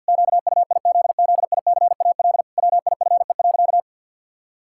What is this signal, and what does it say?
Morse code, heard as a keyed tone, about 50 wpm, 700 Hz. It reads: QUIZZICAL WIFE.
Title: Pangram Challenge v2 - 50wpm